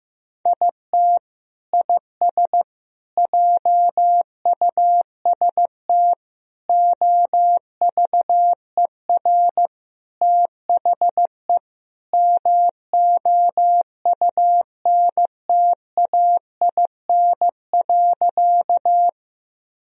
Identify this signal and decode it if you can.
Morse code, heard as a keyed tone, about 15 wpm, 700 Hz. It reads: IT IS JUST OVER THE MOUNTAIN.
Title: Sentences from Top 1000 Words - Review Part 1 - 15wpm